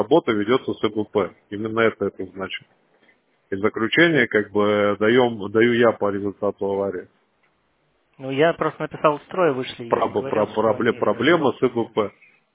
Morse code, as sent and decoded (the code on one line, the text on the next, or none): none